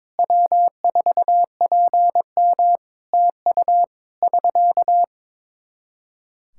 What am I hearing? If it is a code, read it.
Morse code, heard as a keyed tone, about 22 wpm, 700 Hz. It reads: W4PM TU <SK>
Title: Practice QSO - Exchange 4 - Ending - 22wpm